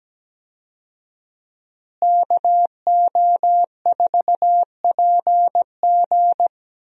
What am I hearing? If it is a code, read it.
Morse code, heard as a keyed tone, about 17 wpm, 700 Hz. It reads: KO4PG
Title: USA Call Signs - 5 Characters - Repeated 3x - 17wpm